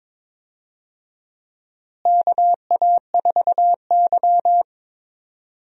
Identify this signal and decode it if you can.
Morse code, heard as a keyed tone, about 22 wpm, 700 Hz. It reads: KA4Y
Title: USA Call Signs - 4 Characters - Repeated 2x - 22wpm